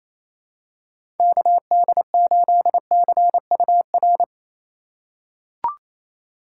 Morse code, heard as a keyed tone, about 28 wpm, 700 Hz. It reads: KD8CUR
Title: USA Call Signs - 6 Characters - 2x Character Spacing - 28wpm